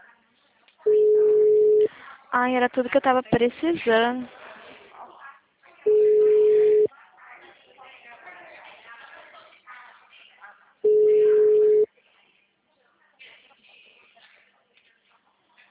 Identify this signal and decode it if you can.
Ringback tone (Europe, 425 Hz), with speech.